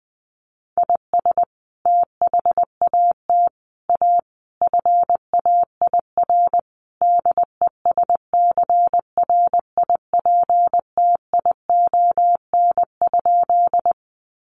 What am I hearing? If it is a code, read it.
Morse code, heard as a keyed tone, about 20 wpm, 700 Hz. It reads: IS THAT A FAIR DESCRIPTION?
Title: Sentences from Top 1300 Words 20wpm